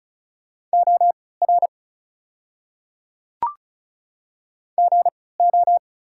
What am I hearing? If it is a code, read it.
Morse code, heard as a keyed tone, about 34 wpm, spaced as for 20 wpm, 690 Hz. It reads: OR GO
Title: 3x Character Spacing - Top 100 Words - 35wpm